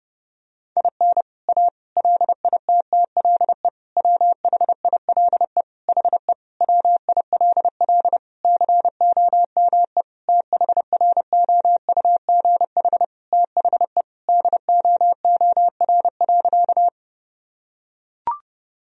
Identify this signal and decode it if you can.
Morse code, heard as a keyed tone, about 32 wpm, 700 Hz. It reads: IN A LITTLE WHILE HE WILL COME THROUGH THE DOOR.
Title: Sentences from Top 1000 Words - Review Part 1 - 30wpm